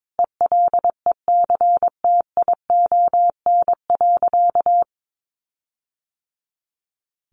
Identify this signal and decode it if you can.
Morse code, heard as a keyed tone, about 22 wpm, 700 Hz. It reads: ELECTION.
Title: Sentences from Top 1500 Words 22wpm